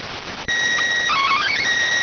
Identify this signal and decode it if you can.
SSTV (slow-scan television), PD 290.